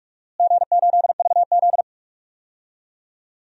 Morse code, heard as a keyed tone, about 42 wpm, 700 Hz. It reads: G8VZ
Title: DXCC-100 Call Signs - 4 Characters - Repeated 3x - 45wpm